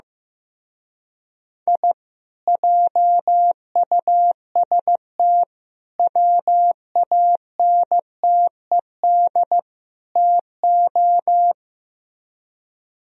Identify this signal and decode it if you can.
Morse code, heard as a keyed tone, about 17 wpm, 700 Hz. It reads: I JUST WANTED TO
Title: Speed-Racing - Sets of 4 Words 15wpm